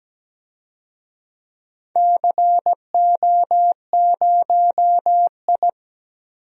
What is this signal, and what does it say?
Morse code, heard as a keyed tone, about 17 wpm, 700 Hz. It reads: CO0I
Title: DXCC-100 Call Signs - 4 Characters - 17wpm